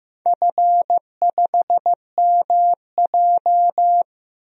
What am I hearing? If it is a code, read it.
Morse code, heard as a keyed tone, about 15 wpm, 700 Hz. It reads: F5MJ
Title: DXCC-100 Call Signs - 4 Characters - Repeated 2x - 15wpm